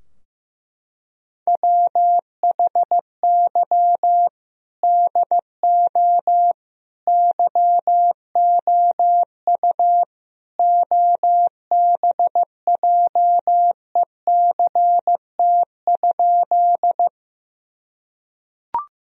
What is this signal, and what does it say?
Morse code, heard as a keyed tone, about 15 wpm, 700 Hz. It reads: WHY DO YOU OBJECT?
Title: Sentences from Top 1000 Words - Review Part 3 - 15wpm